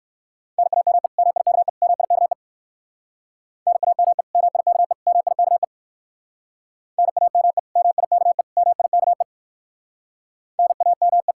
Morse code, heard as a keyed tone, about 56 wpm, 700 Hz. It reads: NAME MIKE MIKE NAME MIKE MIKE NAME MIKE MIKE NAME
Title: Speed-Racing Common QSO Phrases - 45wpm